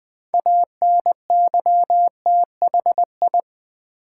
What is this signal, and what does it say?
Morse code, heard as a keyed tone, about 20 wpm, 700 Hz. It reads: ANYTHI
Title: Build A Word - 3 Letters At A Time - 8-Letter Words - 20wpm